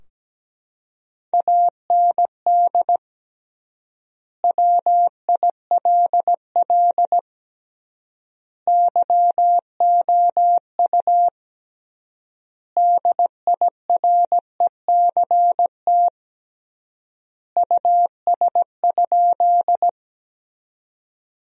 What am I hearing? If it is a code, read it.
Morse code, heard as a keyed tone, about 17 wpm, 700 Hz. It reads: AND WILL YOU DIRECT US?
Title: 3x Word-Spacing - Sentences from Top 400 Words - 17wpm